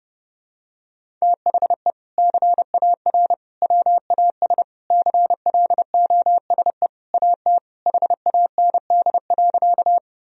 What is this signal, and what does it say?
Morse code, heard as a keyed tone, about 30 wpm, 700 Hz. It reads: THE CAR WAS CLOSE AT HAND.
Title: Sentences from Top 1000 Words - Review Part 1 - 30wpm